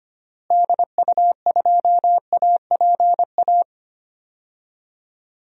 Morse code, heard as a keyed tone, about 25 wpm, 700 Hz. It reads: DU2APA
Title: DXCC-100 Call Signs - 6 Characters - Repeated 3x - 25wpm